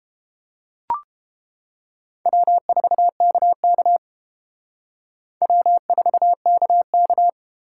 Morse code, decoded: W4KK W4KK

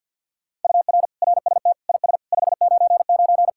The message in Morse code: UR RST IS 599

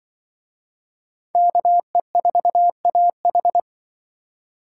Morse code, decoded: KE4AH

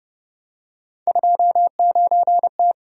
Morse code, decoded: 29T

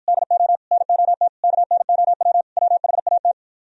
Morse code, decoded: DO NOT KNOW WHAT

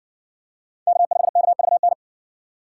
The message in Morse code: K5KVN

